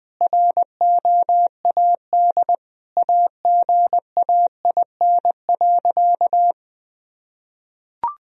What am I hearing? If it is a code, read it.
Morse code, heard as a keyed tone, about 20 wpm, 700 Hz. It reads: ROAD AGAIN.
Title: Sentences from Top 1000 Words - Review Part 2 - 20wpm